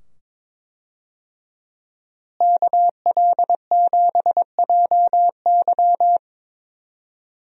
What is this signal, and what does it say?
Morse code, heard as a keyed tone, about 22 wpm, 700 Hz. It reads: KL7JY
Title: DXCC-100 Call Signs - 5 Characters - Repeated 2x - 22wpm